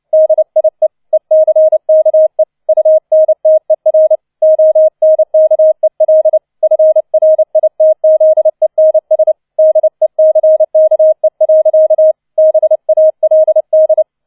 Morse code, heard as a keyed tone, about 29 wpm, 630 Hz. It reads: DIE ECKE UNTER ONKEL FRITZENS DECKE. BALD